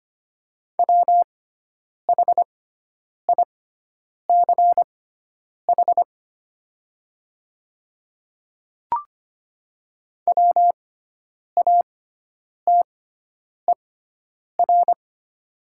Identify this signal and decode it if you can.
Morse code, heard as a keyed tone, about 25 wpm, spaced as for 9 wpm, 700 Hz. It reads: WHICH WATER
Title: 6x Character Spacing - Top 100 Words - 25wpm